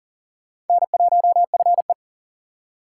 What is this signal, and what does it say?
Morse code, heard as a keyed tone, about 40 wpm, 700 Hz. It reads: N1FE
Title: K1USN SST QSO Practice - 40wpm